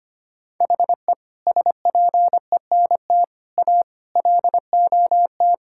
Morse code, heard as a keyed tone, about 25 wpm, 700 Hz. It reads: HE SPENT A LOT
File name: practice / 25wpm / Sentences with 4-Word Phrases - 25wpm